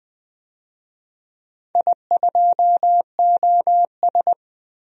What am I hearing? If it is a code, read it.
Morse code, heard as a keyed tone, about 20 wpm, 700 Hz. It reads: I2OS
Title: DXCC-100 Call Signs - 4 Characters - Repeated 3x - 20wpm